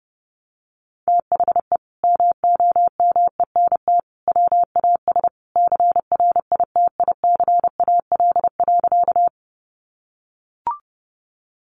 Morse code, heard as a keyed tone, about 30 wpm, 700 Hz. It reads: THE MOMENT WAS CRITICAL.
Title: Sentences from Top 1100 Words 30wpm